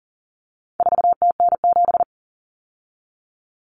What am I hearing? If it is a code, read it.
Morse code, heard as a keyed tone, about 40 wpm, 700 Hz. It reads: 4TN7
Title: Common Cut Numbers - 4-Digit 40wpm